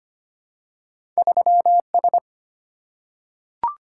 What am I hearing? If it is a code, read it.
Morse code, heard as a keyed tone, about 25 wpm, 700 Hz. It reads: 3S
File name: Two Letter-Number Combinations - 25wpm